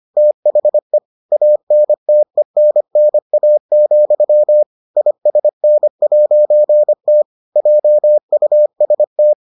Morse code, heard as a keyed tone, about 25 wpm, 600 Hz. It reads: THE ANTENNA, ISN'T JUST